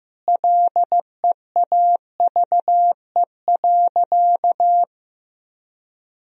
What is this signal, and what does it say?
Morse code, heard as a keyed tone, about 15 wpm, 700 Hz. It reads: LEAVE.
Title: Sentences from Top 1000 Words - Review Part 4 - 15wpm